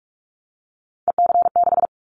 Morse code, heard as a keyed tone, about 46 wpm, 700 Hz. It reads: EC6
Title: Callsign Prefixes for Top 50 DXCC - 45wpm